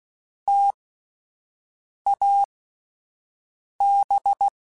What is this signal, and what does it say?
Morse code, heard as a keyed tone, about 16 wpm, spaced as for 5 wpm, 800 Hz. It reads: TAB
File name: Alphabetical / Alphabetical mp3s / Ts @5